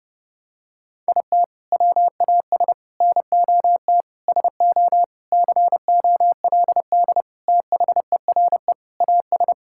Morse code, heard as a keyed tone, about 30 wpm, 700 Hz. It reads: IT WAS NOT SO COLD THERE AS
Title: Sentences from Top 1600 Words 30wpm